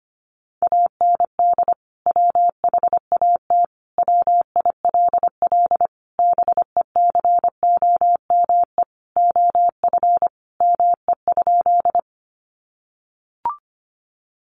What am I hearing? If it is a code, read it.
Morse code, heard as a keyed tone, about 25 wpm, 700 Hz. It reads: AND WHAT WILL BECOME OF ME?
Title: Sentences from Top 600 Words 25wpm